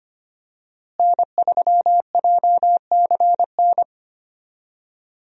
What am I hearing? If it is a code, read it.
Morse code, heard as a keyed tone, about 25 wpm, 700 Hz. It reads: N3JCN
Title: USA Call Signs - 5 Characters - Repeated 2x - 25wpm